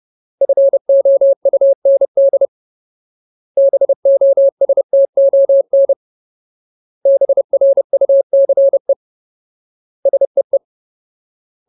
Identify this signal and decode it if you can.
Morse code, heard as a keyed tone, about 30 wpm, 550 Hz. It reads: FOUND BOSTON BRUCE SEE
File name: Ultimate Word List / 30WPM Ultimate